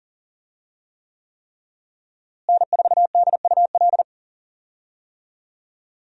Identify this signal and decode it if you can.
Morse code, heard as a keyed tone, about 40 wpm, 700 Hz. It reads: N4DUL
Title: USA Call Signs - 5 Characters - Repeated 3x - 40wpm